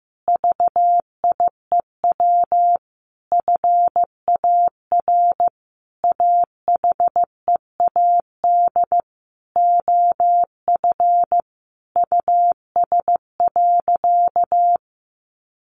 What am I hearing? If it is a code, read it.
Morse code, heard as a keyed tone, about 15 wpm, 700 Hz. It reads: VIEW FAR AHEAD OF US.
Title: Sentences from Top 1100 Words 15wpm